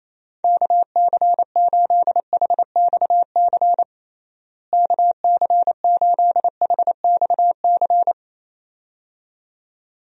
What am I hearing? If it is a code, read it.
Morse code, heard as a keyed tone, about 28 wpm, 700 Hz. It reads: KC8HXC KC8HXC
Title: USA Call Signs - 6 Characters - Repeated 2x - 28wpm